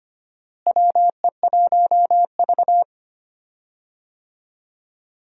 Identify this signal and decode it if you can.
Morse code, heard as a keyed tone, about 25 wpm, 700 Hz. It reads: WE1V
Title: USA Call Signs - 4 Characters - Repeated 3x - 25wpm